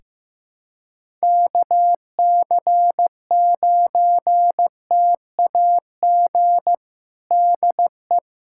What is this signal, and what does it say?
Morse code, heard as a keyed tone, about 15 wpm, 700 Hz. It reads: KC9TAG DE
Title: USA Callsign DE Callsign - 6 Characters - 15wpm